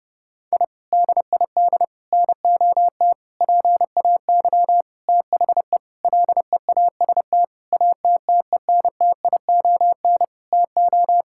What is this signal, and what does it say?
Morse code, heard as a keyed tone, about 30 wpm, 700 Hz. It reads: I DID NOT PAY THE LEAST ATTENTION TO
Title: Sentences from Top 1000 Words - Review Part 6 - 30wpm